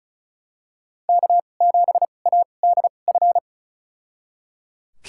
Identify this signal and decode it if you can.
Morse code, heard as a keyed tone, about 34 wpm, spaced as for 25 wpm, 700 Hz. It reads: K7ADF
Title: USA Call Signs - 5 Characters - 2x Character Spacing - 35wpm